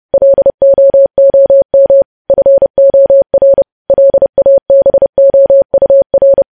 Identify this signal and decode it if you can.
Morse code, heard as a keyed tone, about 30 wpm, 550 Hz. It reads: LOOM FOR LABOUR
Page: news headlines in morse code at 30 wpm